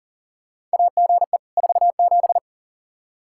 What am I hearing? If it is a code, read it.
Morse code, heard as a keyed tone, about 40 wpm, 700 Hz. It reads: AGE 47
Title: Practice QSO - Exchange 3 - More Info - 40wpm